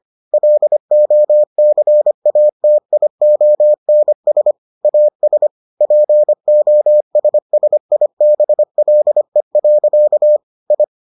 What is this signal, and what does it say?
Morse code, heard as a keyed tone, about 25 wpm, 600 Hz. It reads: LOCATIONS AS POSSIBLE. I